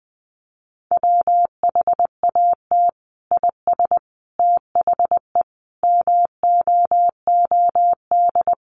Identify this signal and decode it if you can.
Morse code, heard as a keyed tone, about 20 wpm, 700 Hz. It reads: WHAT IS THE MOOD
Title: Sentences from Top 1100 Words 20wpm